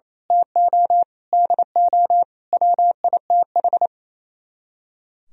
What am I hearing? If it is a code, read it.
Morse code, heard as a keyed tone, about 28 wpm, 700 Hz. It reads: TO DO WITH
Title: Speed-Racing - Sets of 5 Words 28wpm